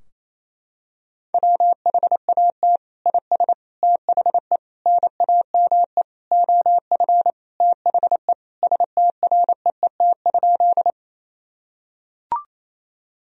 Morse code, heard as a keyed tone, about 28 wpm, 700 Hz. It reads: WHAT IS THE NAME OF THE STREET?
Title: Sentences from Top 1000 Words - Review Part 3 - 28wpm